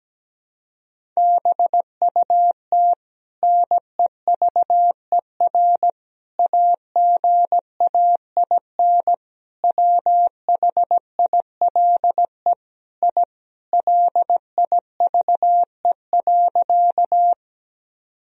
Morse code, heard as a keyed tone, about 17 wpm, 700 Hz. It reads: BUT NEVER AGAIN WHILE I LIVE.
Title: Sentences from Top 1000 Words - Review Part 1 - 17wpm